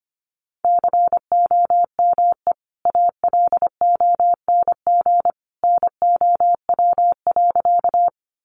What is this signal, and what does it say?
Morse code, heard as a keyed tone, about 25 wpm, 700 Hz. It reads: COME ALONG NOW.
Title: Sentences from Top 700 Words 25wpm